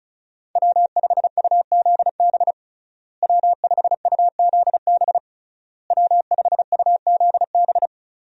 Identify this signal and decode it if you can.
Morse code, heard as a keyed tone, about 36 wpm, 700 Hz. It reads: W5UZB W5UZB W5UZB
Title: USA Call Signs - 5 Characters - Repeated 3x - 35wpm